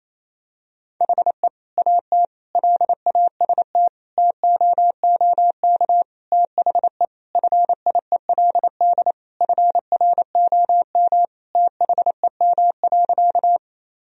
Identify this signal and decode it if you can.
Morse code, heard as a keyed tone, about 28 wpm, 700 Hz. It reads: HE AT LAST TOOK THE FIELD FROM THEM.
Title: Sentences from Top 1000 Words - Review Part 2 - 28wpm